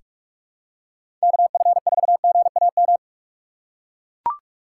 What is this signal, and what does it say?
Morse code, heard as a keyed tone, about 44 wpm, 700 Hz. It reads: KF4GAM
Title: USA Call Signs - 6 Characters - 6x Character Spacing - 45wpm